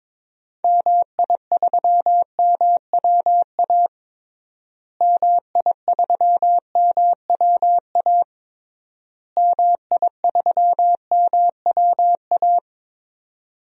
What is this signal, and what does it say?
Morse code, heard as a keyed tone, about 21 wpm, 700 Hz. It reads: MI3MWA MI3MWA MI3MWA
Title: DXCC-100 Call Signs - 6 Characters - Repeated 3x - 22wpm